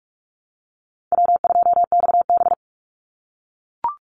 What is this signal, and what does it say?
Morse code, heard as a keyed tone, about 44 wpm, 700 Hz. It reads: W2XB
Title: Easy Call Signs - 45wpm